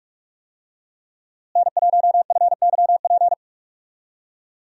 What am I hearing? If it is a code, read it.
Morse code, heard as a keyed tone, about 46 wpm, 700 Hz. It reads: N1FYP
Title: USA Call Signs - 5 Characters - Repeated 2x - 45wpm